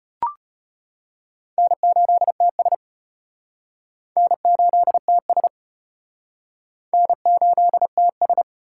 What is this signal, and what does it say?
Morse code, heard as a keyed tone, about 31 wpm, 700 Hz. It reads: N8TS N8TS N8TS
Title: Speed Racing - USA 1x2 Call Signs 25wpm